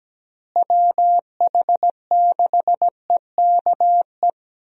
Morse code, heard as a keyed tone, about 17 wpm, 700 Hz. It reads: WH6EKE
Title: USA Call Signs - 6 Characters - Repeated 2x - 17wpm